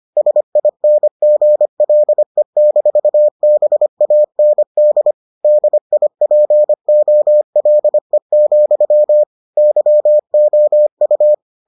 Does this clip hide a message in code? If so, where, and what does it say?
Yes: Morse code from 0.2 to 11.3 s, heard as a keyed tone, about 25 wpm, 600 Hz. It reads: SINGLE-BAND DIPOLE, YOU